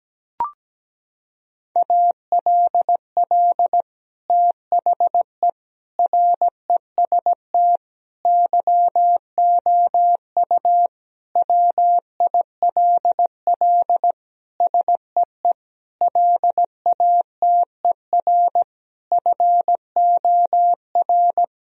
Morse code, heard as a keyed tone, about 17 wpm, 700 Hz. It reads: ALL THE REST YOU WILL SEE LATER FOR
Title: Sentences from Top 1000 Words - Review Part 4 - 17wpm